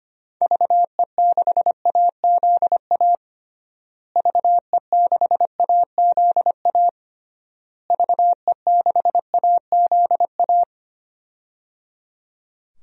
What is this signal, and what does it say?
Morse code, heard as a keyed tone, about 25 wpm, 700 Hz. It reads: VE6AZA VE6AZA VE6AZA
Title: DXCC-100 Call Signs - 6 Characters - Repeated 3x - 25wpm